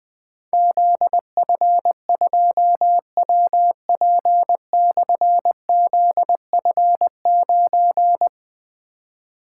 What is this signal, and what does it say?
Morse code, heard as a keyed tone, about 21 wpm, 700 Hz. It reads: ZF2WP/ZF9
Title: Difficult Call Signs 20wpm